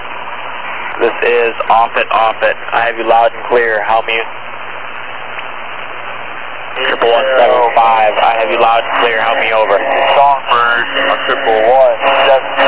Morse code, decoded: none

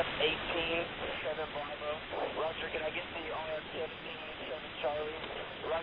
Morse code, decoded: none

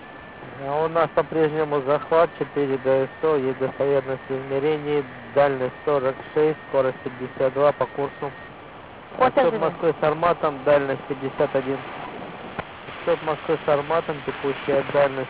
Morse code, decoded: none

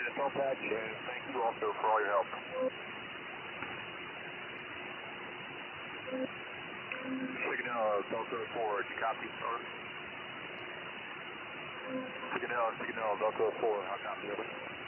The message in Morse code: none

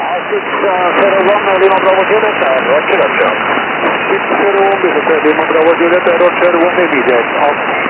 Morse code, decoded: none

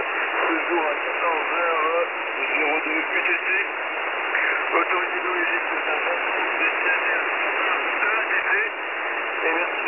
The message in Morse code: none

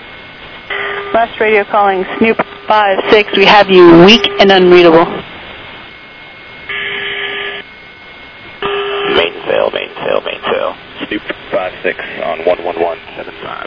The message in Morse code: none

